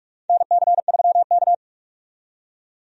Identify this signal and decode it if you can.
Morse code, heard as a keyed tone, about 46 wpm, 700 Hz. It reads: NC3K
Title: USA Call Signs - 4 Characters - 45wpm